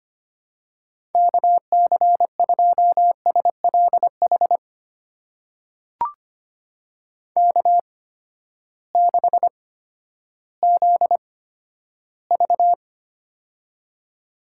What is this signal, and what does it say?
Morse code, heard as a keyed tone, about 25 wpm, spaced as for 12 wpm, 700 Hz. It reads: KC2SLH K6ZV
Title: USA Call Signs - 8x Character Spacing - 25wpm